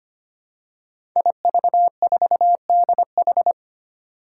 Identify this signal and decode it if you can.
Morse code, heard as a keyed tone, about 25 wpm, 700 Hz. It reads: IV4DH
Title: Five Challenging Letter-Number Combinations 25wpm